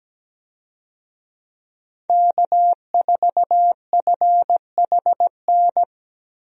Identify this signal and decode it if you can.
Morse code, heard as a keyed tone, about 17 wpm, 700 Hz. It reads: K4FHN
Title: USA Call Signs - 5 Characters - Repeated 2x - 17wpm